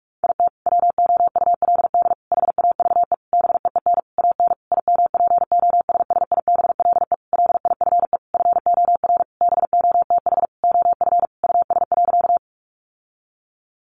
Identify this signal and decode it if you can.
Morse code, heard as a keyed tone, about 44 wpm, 700 Hz. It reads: IT WOULD HAVE BEEN AN IMPOSSIBLE LIFE FOR BOTH OF US.
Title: Sentences from Top 1100 Words 45wpm